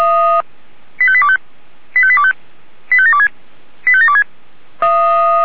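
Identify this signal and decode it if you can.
Five-tone selective call (ZVEI), heard as quick runs of five tones, tones of about 70 ms. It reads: 85825 85825 85726 85726
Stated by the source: Five-tone ZVEI selective call received on SDR.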